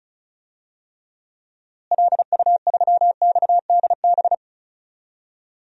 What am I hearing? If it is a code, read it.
Morse code, heard as a keyed tone, about 34 wpm, 700 Hz. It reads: LU3XDB
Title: DXCC-100 Call Signs - 6 Characters - Repeated 3x - 35wpm